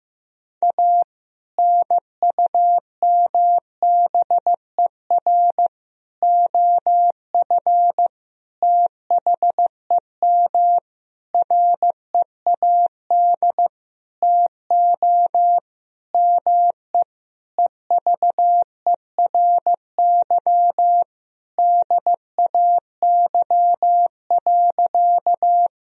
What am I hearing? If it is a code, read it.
Morse code, heard as a keyed tone, about 15 wpm, 700 Hz. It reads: A NUMBER OF THEM READ TO ME EVERY DAY.
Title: Sentences from Top 1000 Words - Review Part 1 - 15wpm